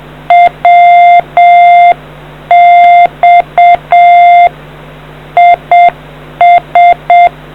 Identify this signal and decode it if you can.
Morse code, heard as a keyed tone, about 7 wpm, 700 Hz. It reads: WX IS